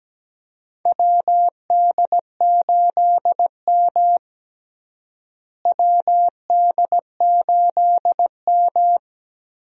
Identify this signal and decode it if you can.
Morse code, heard as a keyed tone, about 17 wpm, 700 Hz. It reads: WD8M WD8M
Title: USA Call Signs - 4 Characters - Repeated 3x - 17wpm